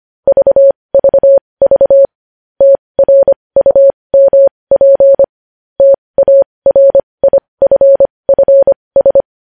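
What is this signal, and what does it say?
Morse code, heard as a keyed tone, about 25 wpm, spaced as for 20 wpm, 550 Hz. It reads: VVV TRUMP TARIFFS
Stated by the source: Morse code news headlines practice at 20 wpm.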